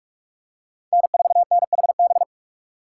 Morse code, heard as a keyed tone, about 46 wpm, 700 Hz. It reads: N4NHB
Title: USA Call Signs - 5 Characters - 2x Character Spacing - 45wpm